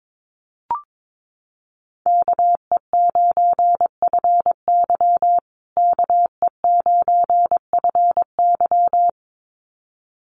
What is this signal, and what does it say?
Morse code, heard as a keyed tone, about 22 wpm, 700 Hz. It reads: KE9FY KE9FY
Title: USA Call Signs Repeated 2x 22wpm